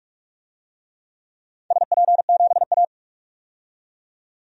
Morse code, heard as a keyed tone, about 46 wpm, 700 Hz. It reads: SP7A
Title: DXCC-100 Call Signs - 4 Characters - 45wpm